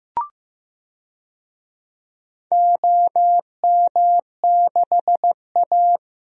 Morse code, heard as a keyed tone, about 15 wpm, 700 Hz. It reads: OM6A
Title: DXCC-100 Call Signs - 4 Characters - Repeated 3x - 15wpm